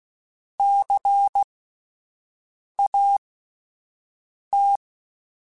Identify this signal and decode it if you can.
Morse code, heard as a keyed tone, about 16 wpm, spaced as for 5 wpm, 800 Hz. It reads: CAT
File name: Spaced Alphas / Spaced Alphas mp3s / Cs sp @10